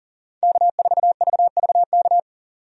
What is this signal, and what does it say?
Morse code, heard as a keyed tone, about 40 wpm, 700 Hz. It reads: K4VVK
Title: USA Call Signs - 5 Characters - 40wpm